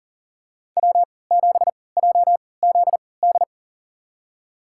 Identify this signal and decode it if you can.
Morse code, heard as a keyed tone, about 40 wpm, spaced as for 23 wpm, 700 Hz. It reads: W7JZD
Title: USA Call Signs - 5 Characters - 3x Character Spacing - 40wpm